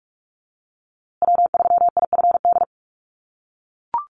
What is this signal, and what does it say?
Morse code, heard as a keyed tone, about 44 wpm, 700 Hz. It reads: W3IFD
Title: USA Call Signs Repeated 2x 45wpm